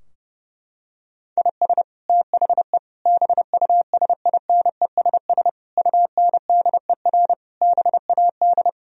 Morse code, heard as a keyed tone, about 32 wpm, 700 Hz. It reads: IS THE BUSINESS UNDER BAD
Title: Sentences from Top 1000 Words - Review Part 4 - 30wpm